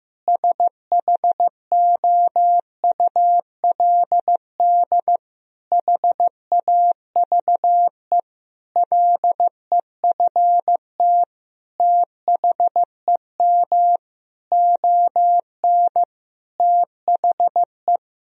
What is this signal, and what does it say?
Morse code, heard as a keyed tone, about 15 wpm, 700 Hz. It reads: SHOULD HAVE LEFT THEM ON THE
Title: Sentences from Top 1000 Words - Review Part 1 - 15wpm